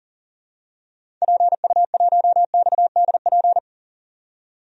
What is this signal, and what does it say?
Morse code, heard as a keyed tone, about 40 wpm, 700 Hz. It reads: PU1XDP
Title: DXCC-100 Call Signs - 6 Characters - 40wpm